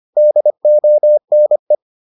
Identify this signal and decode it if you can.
Morse code, heard as a keyed tone, about 25 wpm, 600 Hz. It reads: DONE